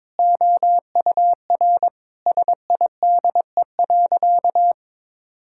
Morse code, heard as a keyed tone, about 22 wpm, 700 Hz. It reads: OUR SIDE.
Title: Sentences from Top 1000 Words - Review Part 6 - 22wpm